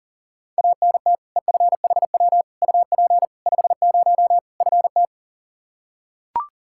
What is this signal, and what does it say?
Morse code, heard as a keyed tone, about 40 wpm, 700 Hz. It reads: ANT EFHW UP 50 FT
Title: Practice QSO - Exchange 2 - Info - 40wpm